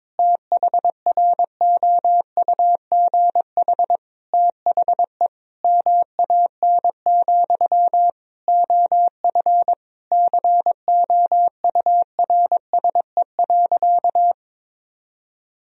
Morse code, heard as a keyed tone, about 22 wpm, 700 Hz. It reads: THROUGH THE MAN, OF COURSE.
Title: Sentences from Top 1000 Words - Review Part 1 - 22wpm